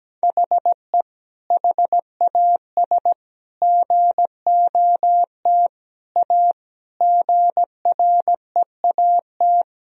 Morse code, heard as a keyed tone, about 17 wpm, 700 Hz. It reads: HE HAS GOT A GREAT
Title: Sentences from Top 1000 Words - Review Part 1 - 17wpm